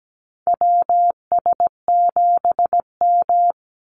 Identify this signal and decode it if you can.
Morse code, heard as a keyed tone, about 17 wpm, 700 Hz. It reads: WS7M
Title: Call Signs - 17wpm